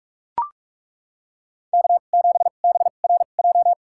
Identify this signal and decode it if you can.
Morse code, heard as a keyed tone, about 42 wpm, spaced as for 31 wpm, 700 Hz. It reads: K7BRJ
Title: USA Call Signs - 2x Character Spacing - 45wpm